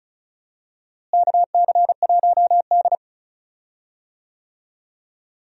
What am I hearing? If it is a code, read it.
Morse code, heard as a keyed tone, about 36 wpm, 700 Hz. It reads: KC1D
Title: USA Call Signs - 4 Characters - Repeated 2x - 35wpm